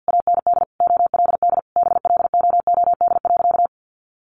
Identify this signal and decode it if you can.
Morse code, heard as a keyed tone, about 49 wpm, 700 Hz. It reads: AND OLD BLOOD.
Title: A Sound Of Thunder 50wpm